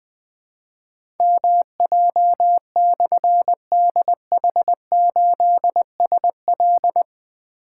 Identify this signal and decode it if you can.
Morse code, heard as a keyed tone, about 20 wpm, 700 Hz. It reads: MJ/DH8SL
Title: Difficult Call Signs 20wpm